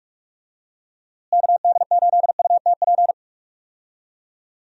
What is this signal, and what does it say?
Morse code, heard as a keyed tone, about 42 wpm, 700 Hz. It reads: KD8UTP
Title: USA Call Signs - 6 Characters - Repeated 2x - 45wpm